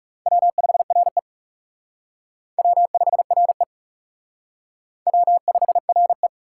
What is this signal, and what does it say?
Morse code, heard as a keyed tone, about 40 wpm, 700 Hz. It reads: W5RE W5RE W5RE